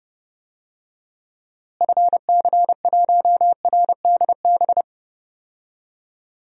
Morse code, heard as a keyed tone, about 30 wpm, 700 Hz. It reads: FC1RDB
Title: DXCC-100 Call Signs - 6 Characters - Repeated 3x - 30wpm